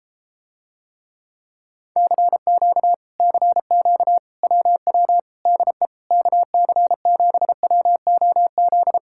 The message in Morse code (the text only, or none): CQ CQ WW DE KC7WOZ